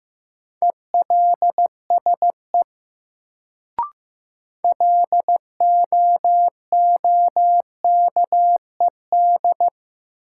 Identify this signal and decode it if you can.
Morse code, heard as a keyed tone, about 15 wpm, 700 Hz. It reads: ELSE LOOKED
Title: Sentences from Top 800 Words - New Word Drill - 15wpm